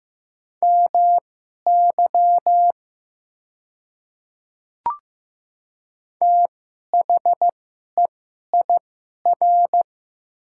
Morse code, heard as a keyed tone, about 15 wpm, spaced as for 11 wpm, 690 Hz. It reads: MY THEIR
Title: 2x Character Spacing - Top 100 Words - 15wpm